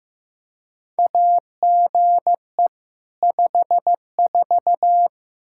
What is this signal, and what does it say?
Morse code, heard as a keyed tone, about 15 wpm, 700 Hz. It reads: AGE 54
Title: Practice QSO - Exchange 3 - More Info - 15wpm